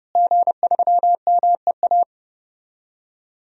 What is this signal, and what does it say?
Morse code, heard as a keyed tone, about 30 wpm, 700 Hz. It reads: G3MEA
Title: DXCC-100 Call Signs - 5 Characters - Repeated 2x - 30wpm